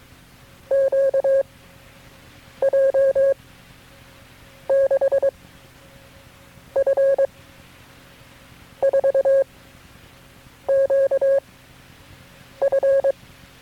Morse code, heard as a keyed tone, about 22 wpm, 540 Hz. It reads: Q J 6 F 4 Q F
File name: FreeFiles / CW-practice / KB6MT-tape3-side1